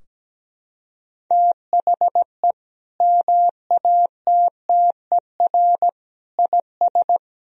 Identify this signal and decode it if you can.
Morse code, heard as a keyed tone, about 17 wpm, 700 Hz. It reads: THE MATTER IS
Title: Sentences from Top 1000 Words - Review Part 4 - 17wpm